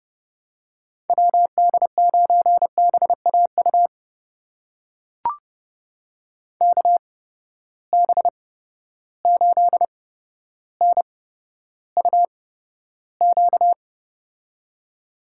Morse code, heard as a keyed tone, about 30 wpm, spaced as for 13 wpm, 700 Hz. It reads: WD9BAU KB8NUQ
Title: USA Call Signs - 6 Characters - 8x Character Spacing - 30wpm